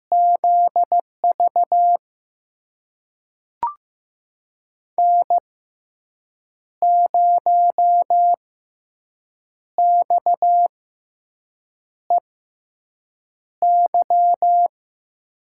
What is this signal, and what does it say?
Morse code, heard as a keyed tone, about 15 wpm, spaced as for 6 wpm, 700 Hz. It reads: ZV N0XEY
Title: USA Call Signs - 5 Characters - 6x Character Spacing - 15wpm